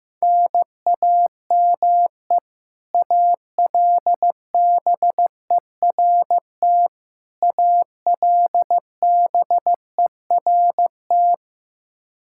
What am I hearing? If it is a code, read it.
Morse code, heard as a keyed tone, about 16 wpm, 700 Hz. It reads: NAME ALBERT ALBERT
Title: Practice QSO - Exchange 1 - Beginning - 15wpm